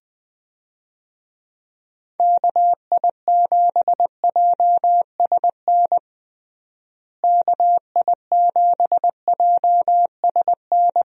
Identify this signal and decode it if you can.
Morse code, heard as a keyed tone, about 21 wpm, 700 Hz. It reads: KI7JSN KI7JSN
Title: USA Call Signs - 6 Characters - Repeated 3x - 20wpm